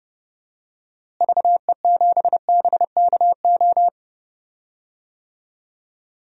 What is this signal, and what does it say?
Morse code, heard as a keyed tone, about 30 wpm, 700 Hz. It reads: VE7BKO
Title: DXCC-100 Call Signs - 6 Characters - Repeated 3x - 30wpm